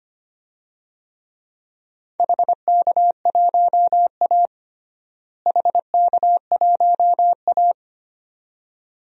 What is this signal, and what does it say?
Morse code, heard as a keyed tone, about 25 wpm, 700 Hz. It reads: HK1A HK1A
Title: DXCC-100 Call Signs - 4 Characters - Repeated 2x - 25wpm